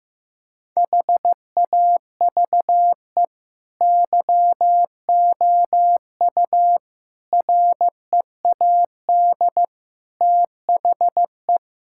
Morse code, heard as a keyed tone, about 15 wpm, 700 Hz. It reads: HAVE YOU READ THE
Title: Sentences from Top 1000 Words - Review Part 5 - 15wpm